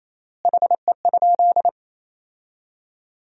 Morse code, heard as a keyed tone, about 28 wpm, 700 Hz. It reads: HE?
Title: 3x Word-Spacing - Sentences from Top 200 Words - 28wpm